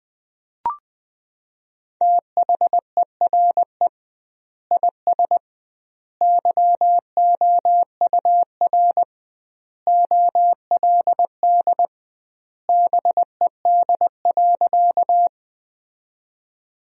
Morse code, heard as a keyed tone, about 20 wpm, 700 Hz. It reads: THERE IS YOUR OLD BED.
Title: 2x Word-Spacing - Sentences from Top 500 Words - 20wpm